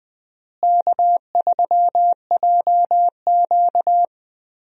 Morse code, heard as a keyed tone, about 20 wpm, 700 Hz. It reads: K3JQ
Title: USA Call Signs - 4 Characters - Repeated 3x - 20wpm